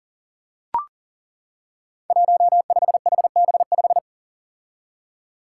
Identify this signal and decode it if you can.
Morse code, heard as a keyed tone, about 39 wpm, 700 Hz. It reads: 15HB5
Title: Five Challenging Letter-Number Combinations 40wpm